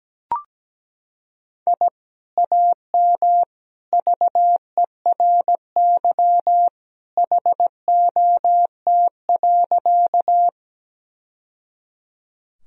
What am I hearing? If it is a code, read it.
Morse code, heard as a keyed tone, about 17 wpm, 700 Hz. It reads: I AM VERY HOT.
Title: Sentences from Top 1000 Words - Review Part 3 - 17wpm